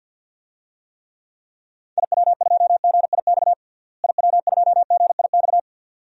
Morse code, heard as a keyed tone, about 51 wpm, 700 Hz. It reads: IW2GIX IW2GIX
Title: DXCC-100 Call Signs - 6 Characters - Repeated 2x - 50wpm